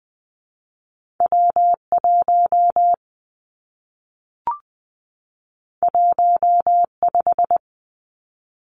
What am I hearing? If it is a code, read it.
Morse code, heard as a keyed tone, about 20 wpm, 700 Hz. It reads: W 1 1 5
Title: Two Challenging Letter-Number Combinations 20wpm